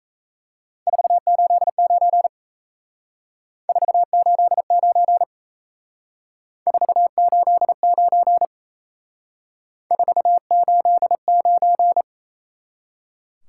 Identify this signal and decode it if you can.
Morse code, heard as a keyed tone, about 35 wpm, 700 Hz. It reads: 489 489 489 489